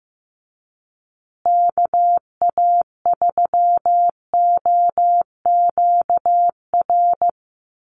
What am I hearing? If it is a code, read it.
Morse code, heard as a keyed tone, about 15 wpm, 700 Hz. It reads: KA3OQR
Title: USA Call Signs Repeated 3x 15wpm